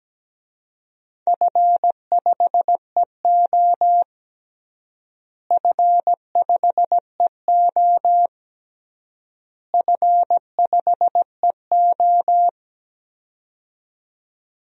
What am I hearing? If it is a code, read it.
Morse code, heard as a keyed tone, about 17 wpm, 700 Hz. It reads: F5EO F5EO F5EO
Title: DXCC-100 Call Signs - 4 Characters - Repeated 3x - 17wpm